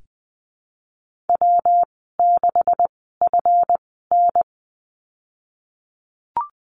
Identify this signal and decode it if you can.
Morse code, heard as a keyed tone, about 20 wpm, spaced as for 15 wpm, 700 Hz. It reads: W6FN
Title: 154 Lesson 25g - Letter p - Call Signs 20wpm